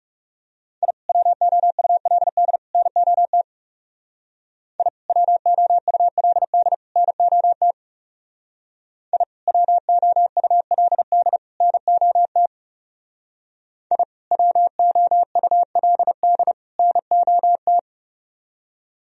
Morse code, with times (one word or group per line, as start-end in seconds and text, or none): 0.8-0.9 I
1.1-2.6 WOULD
2.7-3.4 NOT
4.8-4.9 I
5.1-6.7 WOULD
7.0-7.7 NOT
9.1-9.2 I
9.5-11.4 WOULD
11.6-12.5 NOT
13.9-14.0 I
14.3-16.5 WOULD
16.8-17.8 NOT